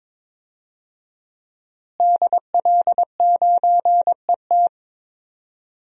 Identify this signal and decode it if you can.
Morse code, heard as a keyed tone, about 22 wpm, 700 Hz. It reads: DL9ET
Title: DXCC-100 Call Signs - 5 Characters - Repeated 3x - 22wpm